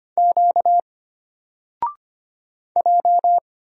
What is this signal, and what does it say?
Morse code, heard as a keyed tone, about 25 wpm, 700 Hz. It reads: Q J
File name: Single-Letter-Number-25wpm